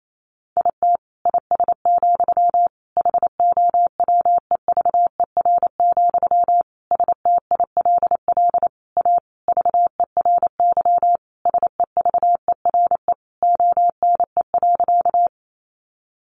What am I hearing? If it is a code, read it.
Morse code, heard as a keyed tone, about 28 wpm, 700 Hz. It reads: IT IS, HOWEVER, STILL A VERY SEVERE ONE.
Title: Sentences from Top 1400 Words 28wpm